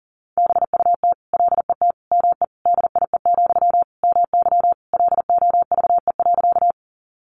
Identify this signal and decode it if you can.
Morse code, heard as a keyed tone, about 40 wpm, 700 Hz. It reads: BUT LET ME DIE, MY LOVE.
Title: Sentences from Top 1300 Words 40wpm